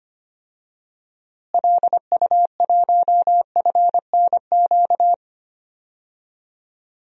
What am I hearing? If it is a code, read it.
Morse code, heard as a keyed tone, about 25 wpm, 700 Hz. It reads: LU1FNQ
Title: DXCC-100 Call Signs - 6 Characters - Repeated 2x - 25wpm